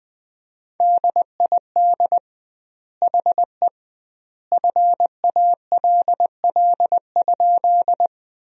Morse code, heard as a keyed tone, about 20 wpm, 700 Hz. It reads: DID HE FALL?
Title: 2x Word-Spacing - Sentences from Top 500 Words - 20wpm